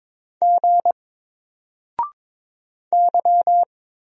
Morse code, heard as a keyed tone, about 22 wpm, 700 Hz. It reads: G Y
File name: Single-Letter-Number-22wpm